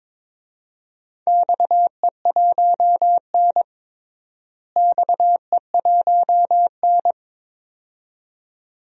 Morse code, heard as a keyed tone, about 22 wpm, 700 Hz. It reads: XE1N XE1N
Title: DXCC-100 Call Signs - 4 Characters - Repeated 2x - 22wpm